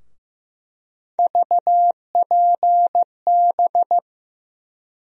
Morse code, heard as a keyed tone, about 15 wpm, 700 Hz. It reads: VPB
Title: USA Call Sign Postfixes - 3 Letters 15wpm